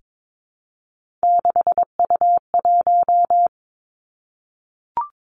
Morse code, heard as a keyed tone, about 22 wpm, 700 Hz. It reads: 6U1
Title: Three Challenging Letter-Number Combinations 22wpm